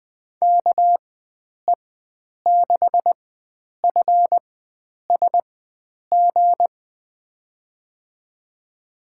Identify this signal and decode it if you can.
Morse code, heard as a keyed tone, about 20 wpm, spaced as for 9 wpm, 700 Hz. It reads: KE6FSG
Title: USA Call Signs - 6 Characters - 4x Character Spacing - 20wpm